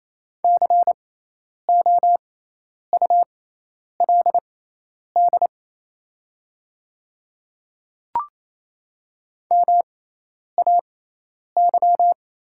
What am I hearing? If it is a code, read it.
Morse code, heard as a keyed tone, about 28 wpm, spaced as for 10 wpm, 700 Hz. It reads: COULD MAY
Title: 6x Character Spacing - Top 100 Words - 28wpm